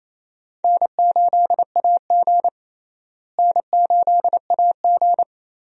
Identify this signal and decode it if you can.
Morse code, heard as a keyed tone, about 28 wpm, 700 Hz. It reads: N8AG N8AG
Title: USA Call Signs - 4 Characters - Repeated 2x - 28wpm